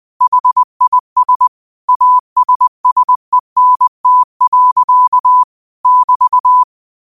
Morse code, heard as a keyed tone, about 20 wpm, 1000 Hz. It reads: HIS ASSENT. <BT>
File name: qotdfiles / 20wpm